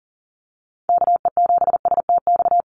Morse code, heard as a keyed tone, about 40 wpm, 700 Hz. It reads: KE7STX
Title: USA Call Signs Repeated 2x 40wpm